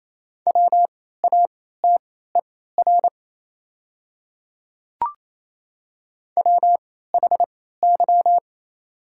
Morse code, heard as a keyed tone, about 28 wpm, spaced as for 16 wpm, 700 Hz. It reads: WATER WHY
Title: 3x Character Spacing - Top 200 Words - 28wpm